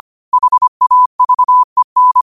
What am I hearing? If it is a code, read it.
Morse code, heard as a keyed tone, about 25 wpm, 1000 Hz. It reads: HAVEN
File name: qotdfiles / 25wpm